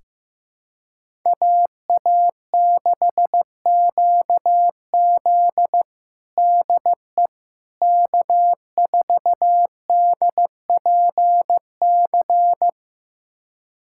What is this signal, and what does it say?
Morse code, heard as a keyed tone, about 15 wpm, 700 Hz. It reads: AA6QZ DE K4DPC
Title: USA Callsign DE Callsign - 5 Characters - 15wpm